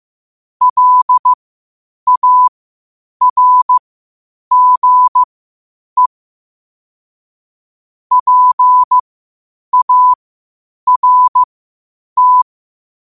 Morse code, heard as a keyed tone, about 15 wpm, spaced as for 8 wpm, 1000 Hz. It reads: LARGE PART
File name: qotdfiles / 8wpm